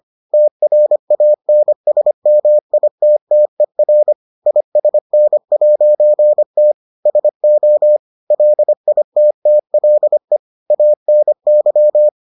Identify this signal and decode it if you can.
Morse code, heard as a keyed tone, about 25 wpm, 600 Hz. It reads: TRANSMITTER ISN'T SO LITTLE ANY